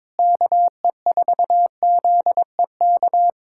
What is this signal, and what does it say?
Morse code, heard as a keyed tone, about 22 wpm, 700 Hz. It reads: KE4ZEK
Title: USA Callsign DE Callsign - 6 Characters - 22wpm